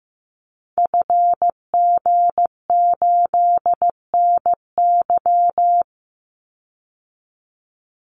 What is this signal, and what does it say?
Morse code, heard as a keyed tone, about 15 wpm, 700 Hz. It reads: FG8NY
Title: Call Signs Repeated 2x - 15wpm